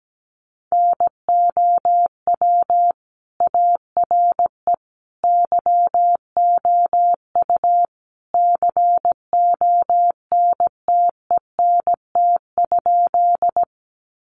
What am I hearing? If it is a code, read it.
Morse code, heard as a keyed tone, about 17 wpm, 700 Hz. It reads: NOW ARE YOU CONTENT?
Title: Sentences from Top 900 Words 17wpm